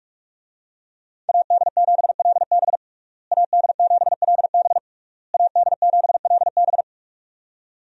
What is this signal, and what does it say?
Morse code, heard as a keyed tone, about 46 wpm, 700 Hz. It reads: AD7LB AD7LB AD7LB
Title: USA Call Signs - 5 Characters - Repeated 3x - 45wpm